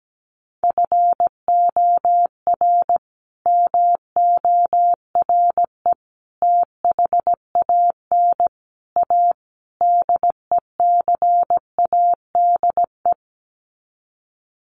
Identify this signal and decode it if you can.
Morse code, heard as a keyed tone, about 17 wpm, 700 Hz. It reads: FOR MORE THAN A DECADE
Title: Sets of 5 Words 17wpm